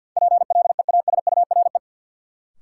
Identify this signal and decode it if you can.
Morse code, heard as a keyed tone, about 53 wpm, 700 Hz. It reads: PLEASURE
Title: Sentences from Top 1600 Words - New Word Drill - 50wpm